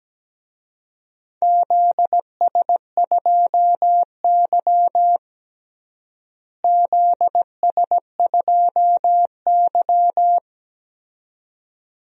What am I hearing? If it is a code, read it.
Morse code, heard as a keyed tone, about 17 wpm, 700 Hz. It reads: ZS2Y ZS2Y
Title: DXCC-100 Call Signs - 4 Characters - Repeated 2x - 17wpm